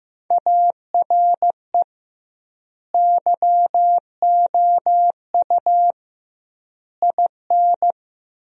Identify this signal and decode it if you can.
Morse code, heard as a keyed tone, about 15 wpm, 700 Hz. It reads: ARE YOU IN
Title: 2x Word-Spacing - Sentences from Top 400 Words - 15wpm